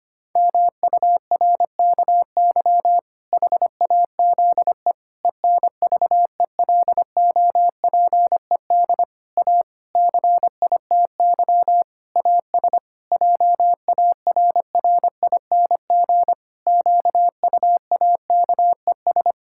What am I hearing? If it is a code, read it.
Morse code, heard as a keyed tone, about 25 wpm, 700 Hz. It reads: MURKY HAZE ENVELOPED A CITY AS JARRING QUAKES
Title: Pangram Challenge v2 - 25wpm